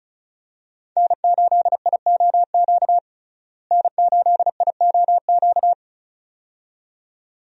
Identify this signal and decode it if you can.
Morse code, heard as a keyed tone, about 36 wpm, 700 Hz. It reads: N8IOQ N8IOQ
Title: USA Call Signs - 5 Characters - Repeated 2x - 35wpm